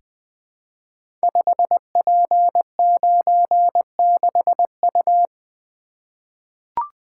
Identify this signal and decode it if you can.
Morse code, heard as a keyed tone, about 20 wpm, 700 Hz. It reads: 5P96U
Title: Code Groups - 5 Letter:Numbers - 20wpm